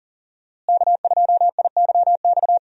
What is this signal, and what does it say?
Morse code, heard as a keyed tone, about 39 wpm, 700 Hz. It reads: K2IYX
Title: USA Call Signs - 5 Characters - Repeated 2x - 40wpm